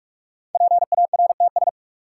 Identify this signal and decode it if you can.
Morse code, heard as a keyed tone, about 46 wpm, 700 Hz. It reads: PARTS
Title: Sets of 3 Words - New Word Drill - 45wpm